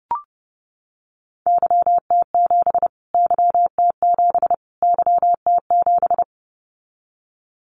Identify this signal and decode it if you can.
Morse code, heard as a keyed tone, about 30 wpm, 700 Hz. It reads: YT7 YT7 YT7
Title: Callsign Prefixes for Top 25 DXCC - Repeated 3x - 30wpm